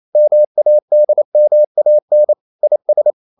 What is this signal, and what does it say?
Morse code, heard as a keyed tone, about 28 wpm, 600 Hz. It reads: MADMAN IS
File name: CW / Practice / EnglishQuotations0128